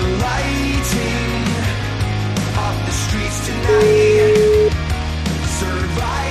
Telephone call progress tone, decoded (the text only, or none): ringback tone